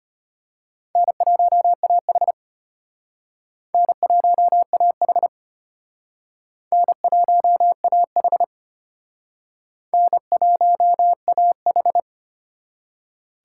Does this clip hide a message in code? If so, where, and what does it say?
Yes: Morse code from 1.0 to 12.0 s, heard as a keyed tone, about 34 wpm, 700 Hz. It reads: N1AH N1AH N1AH N1AH